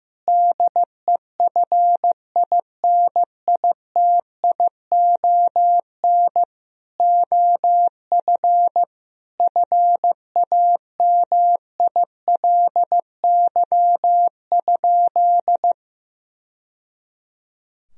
Morse code, heard as a keyed tone, about 15 wpm, 700 Hz. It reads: DEFINITION OF FAMILY?
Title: Sentences from Top 1000 Words - Review Part 4 - 15wpm